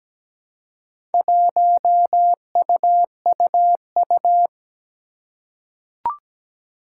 Morse code, heard as a keyed tone, about 17 wpm, 700 Hz. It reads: 1UUU
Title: Four Challenging Letter-Number Combinations 17wpm